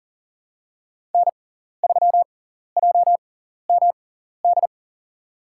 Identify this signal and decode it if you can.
Morse code, heard as a keyed tone, about 40 wpm, spaced as for 14 wpm, 700 Hz. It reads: N3JMD
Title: USA Call Signs - 5 Characters - 6x Character Spacing - 40wpm